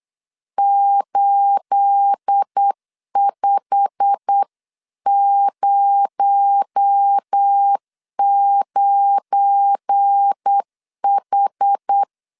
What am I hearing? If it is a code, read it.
Morse code, heard as a keyed tone, about 8 wpm, 800 Hz. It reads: 8509H
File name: CW / DET / 1 9 1500C S C 8wpm 5653